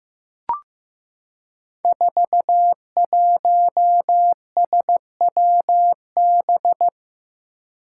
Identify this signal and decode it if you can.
Morse code, heard as a keyed tone, about 16 wpm, 700 Hz. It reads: 41SWB
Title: Five Challenging Letter-Number Combinations 15wpm